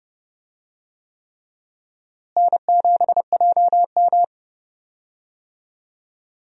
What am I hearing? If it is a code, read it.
Morse code, heard as a keyed tone, about 30 wpm, 700 Hz. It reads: N7JM
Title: USA Call Signs - 4 Characters - Repeated 3x - 30wpm